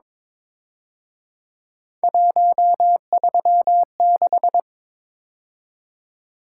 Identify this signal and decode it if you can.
Morse code, heard as a keyed tone, about 26 wpm, 700 Hz. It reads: 136
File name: Speed-Racing - Three Letter-Number Combinations - 22wpm